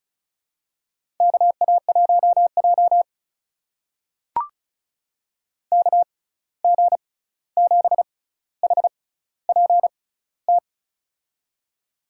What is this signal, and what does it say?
Morse code, heard as a keyed tone, about 34 wpm, spaced as for 16 wpm, 700 Hz. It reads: KA1J KG7HPT
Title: USA Call Signs - 6x Character Spacing - 35wpm